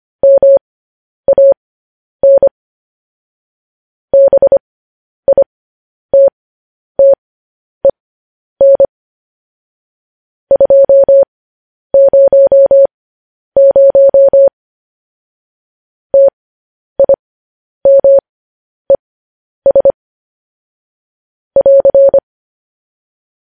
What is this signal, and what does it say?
Morse code, heard as a keyed tone, about 25 wpm, spaced as for 10 wpm, 550 Hz. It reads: MAN BITTEN 200 TIMES <AR>
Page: news-headlines-in-morse-code-at-10-wpm